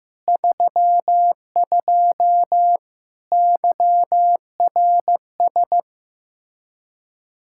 Morse code, heard as a keyed tone, about 15 wpm, 700 Hz. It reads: 32 YRS
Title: Practice QSO - Exchange 3 - More Info - 15wpm